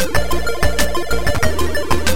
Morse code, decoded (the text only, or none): none